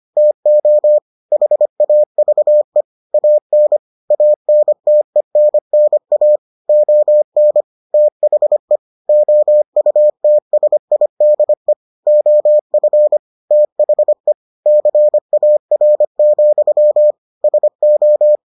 Morse code, heard as a keyed tone, about 25 wpm, 600 Hz. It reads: TO HAVE AN ANTENNA ON THE OUTSIDE OF THE CAR, SO